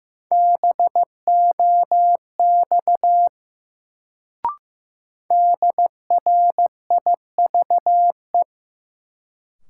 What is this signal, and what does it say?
Morse code, heard as a keyed tone, about 15 wpm, 700 Hz. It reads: BOX DRIVE
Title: Sentences from Top 500 Words - New Word Drill - 15wpm